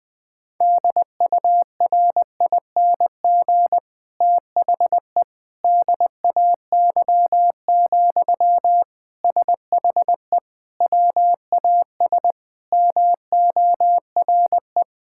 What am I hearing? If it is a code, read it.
Morse code, heard as a keyed tone, about 20 wpm, 700 Hz. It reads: DURING THE DAY, SHE WAS MORE
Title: Sentences from Top 1000 Words - Review Part 6 - 20wpm